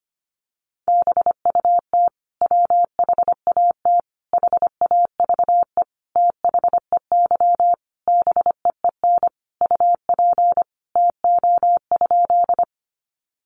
Morse code, heard as a keyed tone, about 25 wpm, 700 Hz. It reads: BUT WHAT HAVE THEY BEEN UP TO?
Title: Sentences from Top 100 Words 25wpm